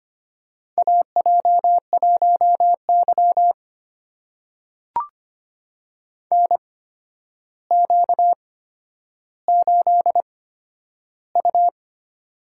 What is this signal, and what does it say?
Morse code, heard as a keyed tone, about 25 wpm, spaced as for 11 wpm, 700 Hz. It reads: AJ1Y NQ8U
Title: USA Call Signs - 4 Characters - 8x Character Spacing - 25wpm